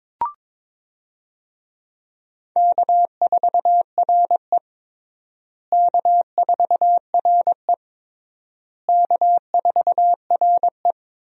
Morse code, heard as a keyed tone, about 21 wpm, 700 Hz. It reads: K4RE K4RE K4RE
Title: USA Call Signs - 4 Characters - Repeated 3x - 22wpm